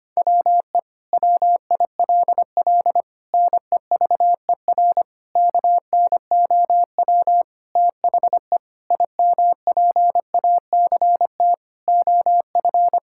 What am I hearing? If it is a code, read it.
Morse code, heard as a keyed tone, about 25 wpm, 700 Hz. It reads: WE WILL NEVER KNOW THE IMPACT OF
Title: Sentences from Top 1000 Words - Review Part 6 - 25wpm